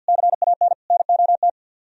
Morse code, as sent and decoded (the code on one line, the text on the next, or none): -.-. .- -. / -. --- -
CAN NOT